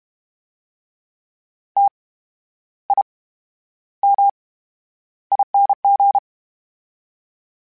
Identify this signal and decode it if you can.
Morse code, heard as a keyed tone, about 32 wpm, spaced as for 11 wpm, 800 Hz. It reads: TIMING
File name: Special INGs / Special INGs-mp3s / VerySlowToFaster Ing @30